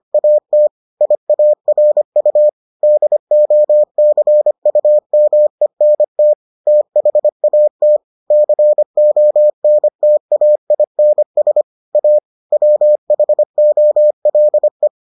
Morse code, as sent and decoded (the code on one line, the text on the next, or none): .- - / .. .- .-. ..- / -.. --- -.-. ..- -- . -. - / - .... .- - / -.-. --- -. - .- .. -. ... / .- / .-- .... --- .-.. .
AT IARU DOCUMENT THAT CONTAINS A WHOLE